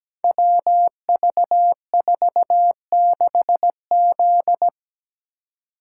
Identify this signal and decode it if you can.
Morse code, heard as a keyed tone, about 17 wpm, 700 Hz. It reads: WV46Z
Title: Code Groups - 5 Letter:Numbers - 17wpm